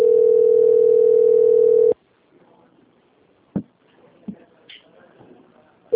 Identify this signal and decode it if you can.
Ringback tone (North America), 440 + 480 Hz.
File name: out- - - - - .38668